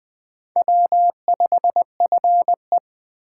Morse code, heard as a keyed tone, about 20 wpm, 700 Hz. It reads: W5FE
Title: Speed Racing - USA Call Signs 17wpm